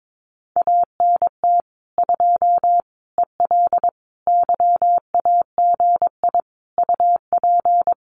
Morse code, heard as a keyed tone, about 22 wpm, 700 Hz. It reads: ANT 2 EL YAGI UP
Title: Common QSO Phrases Encore 22wpm